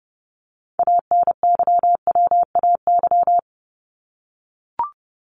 Morse code, heard as a keyed tone, about 28 wpm, 700 Hz. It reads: ANYWAY
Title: Top 1501-1750 Words 30wpm